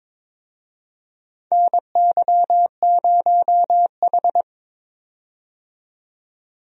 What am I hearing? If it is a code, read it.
Morse code, heard as a keyed tone, about 22 wpm, 700 Hz. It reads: NY0H
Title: USA Call Signs - 4 Characters - 22wpm